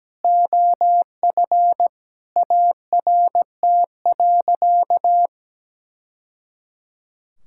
Morse code, heard as a keyed tone, about 17 wpm, 700 Hz. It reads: OF ART.
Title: Sentences from Top 1000 Words - Review Part 6 - 17wpm